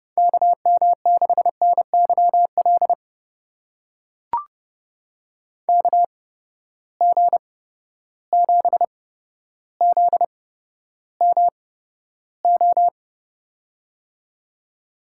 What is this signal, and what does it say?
Morse code, heard as a keyed tone, about 30 wpm, spaced as for 13 wpm, 700 Hz. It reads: KM6NYL KG7ZMO
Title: USA Call Signs - 6 Characters - 8x Character Spacing - 30wpm